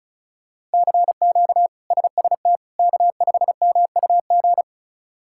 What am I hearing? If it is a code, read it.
Morse code, heard as a keyed tone, about 34 wpm, 700 Hz. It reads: CQ SST K5MUG
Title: K1USN SST QSO Practice - 35wpm